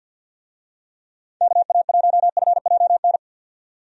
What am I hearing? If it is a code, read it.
Morse code, heard as a keyed tone, about 51 wpm, 700 Hz. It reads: KA1FJN
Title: USA Call Signs - 6 Characters - 50wpm